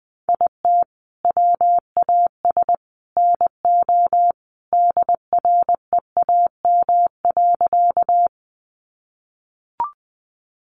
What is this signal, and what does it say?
Morse code, heard as a keyed tone, about 20 wpm, 700 Hz. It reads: IT WAS NO DREAM.
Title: Sentences from Top 1500 Words 20wpm